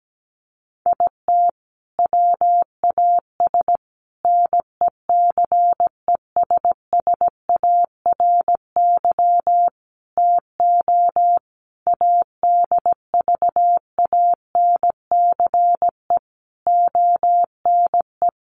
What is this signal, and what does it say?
Morse code, heard as a keyed tone, about 17 wpm, 700 Hz. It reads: IT WAS NECESSARY TO ADVANCE ONE
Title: Sentences from Top 1400 Words 17wpm